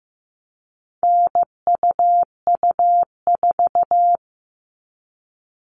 Morse code, heard as a keyed tone, about 15 wpm, 700 Hz. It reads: NUU4
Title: All Cut Numbers - 4-Digit 15wpm